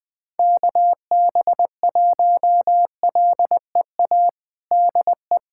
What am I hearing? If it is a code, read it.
Morse code, heard as a keyed tone, about 20 wpm, 700 Hz. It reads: KB1LEA DE
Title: USA Callsign DE Callsign 20wpm